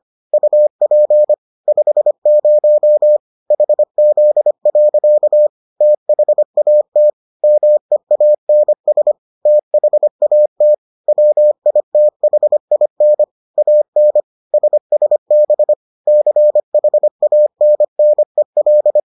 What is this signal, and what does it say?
Morse code, heard as a keyed tone, about 25 wpm, 600 Hz. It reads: UP 50 HZ. THAT MEANS THAT WITHIN AN SSB CHANNEL